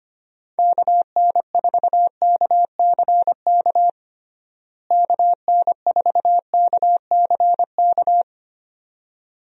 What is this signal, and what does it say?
Morse code, heard as a keyed tone, about 25 wpm, 700 Hz. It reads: KN4KCK KN4KCK
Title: USA Call Signs - 6 Characters - Repeated 2x - 25wpm